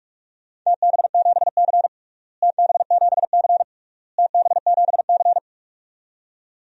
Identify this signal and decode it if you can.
Morse code, heard as a keyed tone, about 41 wpm, 700 Hz. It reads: TB7C TB7C TB7C
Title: DXCC-100 Call Signs - 4 Characters - Repeated 3x - 45wpm